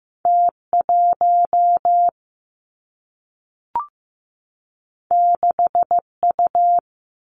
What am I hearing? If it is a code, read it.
Morse code, heard as a keyed tone, about 15 wpm, 700 Hz. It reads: T 1 6 U
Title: Two Challenging Letter-Number Combinations 15wpm